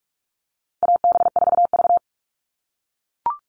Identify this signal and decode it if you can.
Morse code, heard as a keyed tone, about 44 wpm, 700 Hz. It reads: AB4V
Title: All Cut Numbers - 4-Digit 45wpm